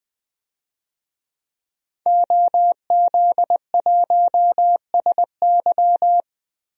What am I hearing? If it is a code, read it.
Morse code, heard as a keyed tone, about 20 wpm, 700 Hz. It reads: OZ1SY
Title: DXCC-100 Call Signs - 5 Characters - Repeated 2x - 20wpm